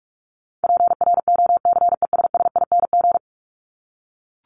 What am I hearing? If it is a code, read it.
Morse code, heard as a keyed tone, about 44 wpm, 700 Hz. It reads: PROCESSING
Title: ING Suffix - Top 100 Words with Base Words 45wpm